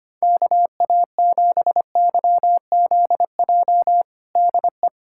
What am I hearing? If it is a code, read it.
Morse code, heard as a keyed tone, about 25 wpm, 700 Hz. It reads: KA7YZJ DE
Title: USA Callsign DE Callsign - 6 Characters - 25wpm